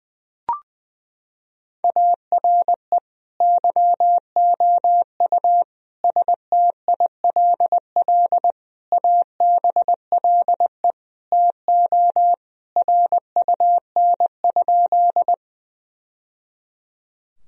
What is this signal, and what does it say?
Morse code, heard as a keyed tone, about 21 wpm, 700 Hz. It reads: ARE YOU STILL ABLE TO RUN?
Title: Sentences from Top 1000 Words - Review Part 2 - 20wpm